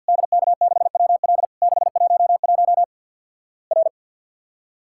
Morse code, heard as a keyed tone, about 50 wpm, 700 Hz. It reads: DK6WL 611 R
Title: Worked All Europe DX Contest - QTC Exchange - 50wpm